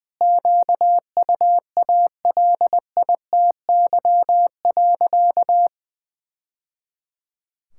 Morse code, heard as a keyed tone, about 21 wpm, 700 Hz. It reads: QUALITY.
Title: Sentences from Top 1000 Words - Review Part 5 - 20wpm